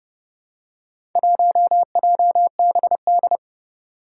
Morse code, heard as a keyed tone, about 31 wpm, 700 Hz. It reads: 1JBD
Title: Four Challenging Letter-Number Combinations 30wpm